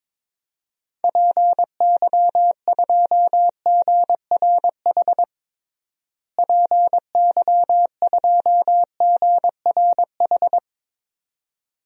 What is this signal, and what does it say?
Morse code, heard as a keyed tone, about 22 wpm, 700 Hz. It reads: PY2GRH PY2GRH
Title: DXCC-100 Call Signs - 6 Characters - Repeated 2x - 22wpm